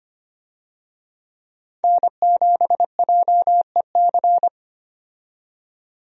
Morse code, heard as a keyed tone, about 25 wpm, 700 Hz. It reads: N7JEC
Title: USA Call Signs - 5 Characters - 25wpm